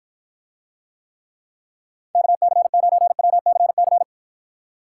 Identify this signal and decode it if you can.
Morse code, heard as a keyed tone, about 49 wpm, 700 Hz. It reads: KC9WCC